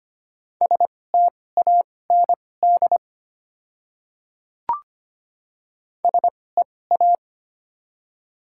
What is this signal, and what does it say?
Morse code, heard as a keyed tone, about 25 wpm, spaced as for 18 wpm, 690 Hz. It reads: STAND SEA
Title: 2x Character Spacing - Top 300 Words - 25wpm